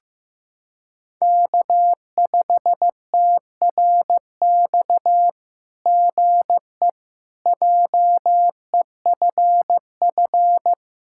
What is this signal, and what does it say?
Morse code, heard as a keyed tone, about 15 wpm, 700 Hz. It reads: K5TRX GE JEFF
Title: SOTA QSO Practice 15wpm